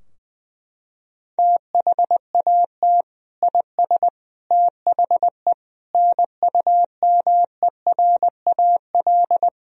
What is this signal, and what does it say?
Morse code, heard as a keyed tone, about 20 wpm, 700 Hz. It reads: THAT IS THE NUMERAL
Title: Sentences from Top 1000 Words - Review Part 2 - 20wpm